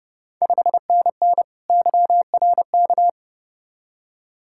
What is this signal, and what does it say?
Morse code, heard as a keyed tone, about 30 wpm, 690 Hz. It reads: 5NN YRK
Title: Ontario QSO Party Exchange 30wpm